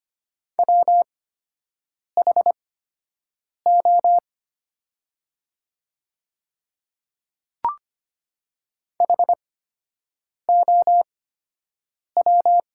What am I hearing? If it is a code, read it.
Morse code, heard as a keyed tone, about 25 wpm, spaced as for 7 wpm, 700 Hz. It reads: WHO HOW
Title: 8x Character Spacing - Top 100 Words - 25wpm